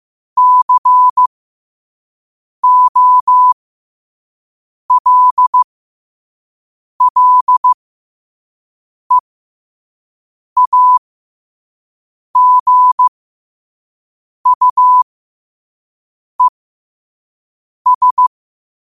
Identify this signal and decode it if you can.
Morse code, heard as a keyed tone, about 15 wpm, spaced as for 5 wpm, 1000 Hz. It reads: COLLEAGUES
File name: qotdfiles / 5wpm